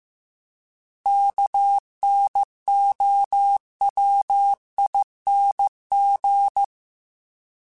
Morse code, heard as a keyed tone, about 15 wpm, 800 Hz. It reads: KNOWING